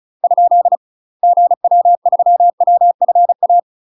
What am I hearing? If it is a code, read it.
Morse code, heard as a keyed tone, about 34 wpm, 700 Hz. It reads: ? GW3WFA
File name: UK MASTER QSOs TRAINING / 35WPM / UK MASTER QSOs 0015